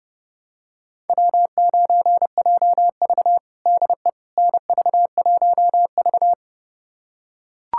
Morse code, heard as a keyed tone, about 30 wpm, 700 Hz. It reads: W9JV DE NV1V
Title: USA Callsign DE Callsign - 4 Characters - 30wpm